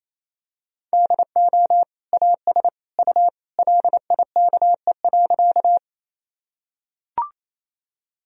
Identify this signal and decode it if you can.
Morse code, heard as a keyed tone, about 28 wpm, 700 Hz. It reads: DO AS U LIKE.
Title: Sentences from Top 100 Words and Abbreviations 28wpm